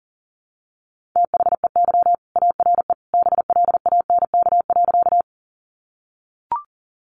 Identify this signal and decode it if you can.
Morse code, heard as a keyed tone, about 40 wpm, 700 Hz. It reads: THEY ARE BLANK.
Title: Sentences from Top 1500 Words 40wpm